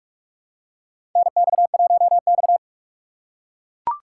Morse code, heard as a keyed tone, about 43 wpm, 700 Hz. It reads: NX1X
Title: USA Call Signs - 4 Characters - 2x Character Spacing - 45wpm